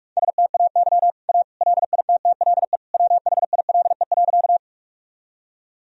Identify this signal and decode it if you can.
Morse code, heard as a keyed tone, about 46 wpm, 700 Hz. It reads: STAY A LITTLE WHILE.
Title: Sentences from Top 1000 Words - Review Part 1 - 45wpm